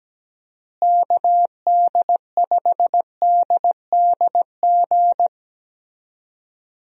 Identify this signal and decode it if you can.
Morse code, heard as a keyed tone, about 17 wpm, 700 Hz. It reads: KD5DDG
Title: USA Call Signs - 6 Characters - Repeated 2x - 17wpm